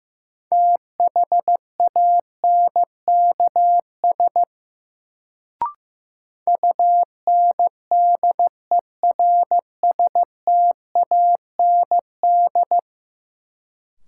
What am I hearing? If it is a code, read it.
Morse code, heard as a keyed tone, about 15 wpm, 700 Hz. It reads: THANKS UNDERSTAND
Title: Sentences from Top 600 Words - New Word Drill - 15wpm